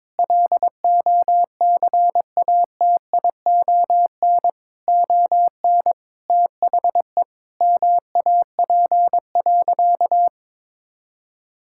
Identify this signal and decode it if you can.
Morse code, heard as a keyed tone, about 22 wpm, 700 Hz. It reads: LOCATION ON THE MAP.
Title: Sentences from Top 1000 Words - Review Part 6 - 22wpm